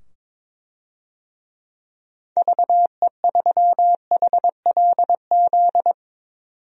Morse code, heard as a keyed tone, about 22 wpm, 700 Hz. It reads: VE3HLZ
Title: DXCC-100 Call Signs - 6 Characters - 22wpm